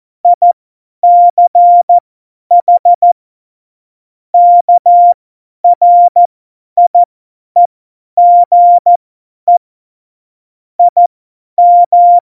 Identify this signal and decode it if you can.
Morse code, heard as a keyed tone, about 14 wpm, spaced as for 10 wpm, 700 Hz. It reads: ICH KRIEGE IM
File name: Morsekurs / Witze / Witz 24. Oktober 17, Tempo 50